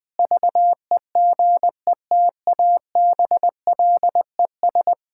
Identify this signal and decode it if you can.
Morse code, heard as a keyed tone, about 20 wpm, 700 Hz. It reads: VEGETABLES
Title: Difficult Sentences with Hints - 20wpm